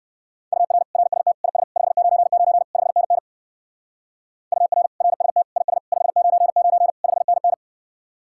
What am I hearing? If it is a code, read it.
Morse code, heard as a keyed tone, about 58 wpm, 700 Hz. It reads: UR RST IS 599 5NN UR RST IS 599 5NN